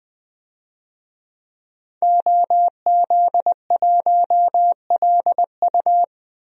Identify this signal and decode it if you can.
Morse code, heard as a keyed tone, about 20 wpm, 700 Hz. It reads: OZ1LU
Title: DXCC-100 Call Signs - 5 Characters - Repeated 2x - 20wpm